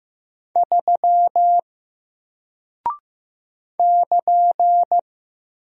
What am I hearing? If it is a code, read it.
Morse code, heard as a keyed tone, about 15 wpm, 700 Hz. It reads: 3 <KN>
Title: Single Character (Letter:Number:Punctuation:Prosign) Alt 15wpm